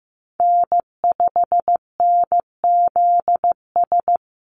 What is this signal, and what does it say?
Morse code, heard as a keyed tone, about 15 wpm, 700 Hz. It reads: N5NZS
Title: USA Call Signs Repeated 3x 15wpm